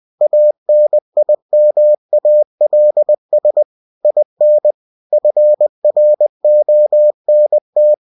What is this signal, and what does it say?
Morse code, heard as a keyed tone, about 20 wpm, 600 Hz. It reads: ANIMALS IN FRONT
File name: cw-books / winnie-the-pooh / winnie-20wpm-page-0023